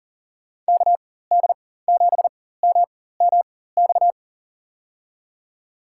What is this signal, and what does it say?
Morse code, heard as a keyed tone, about 40 wpm, spaced as for 19 wpm, 700 Hz. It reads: KD7MMX
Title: USA Call Signs - 6 Characters - 4x Character Spacing - 40wpm